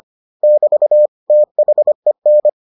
Morse code, heard as a keyed tone, about 25 wpm, 600 Hz. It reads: <BT> THEN